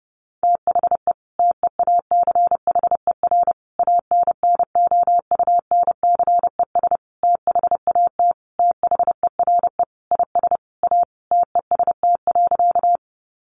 Morse code, heard as a keyed tone, about 30 wpm, 700 Hz. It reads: THE TEACHER ANNOUNCES THAT THERE IS A TEST.
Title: Additional Easy Phrases 30wpm